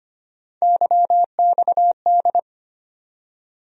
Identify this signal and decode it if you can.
Morse code, heard as a keyed tone, about 25 wpm, 700 Hz. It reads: YXD
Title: USA Call Sign Postfixes - 3 Letters 25wpm